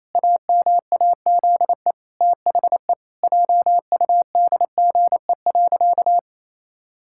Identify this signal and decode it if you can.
Morse code, heard as a keyed tone, about 28 wpm, 700 Hz. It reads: AMAZE THE JUDGE.
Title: Pangram Challenge v2 - 28wpm